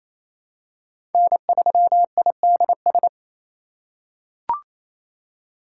Morse code, heard as a keyed tone, about 28 wpm, 700 Hz. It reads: N3IDS
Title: USA Call Signs - 2x Character Spacing - 28wpm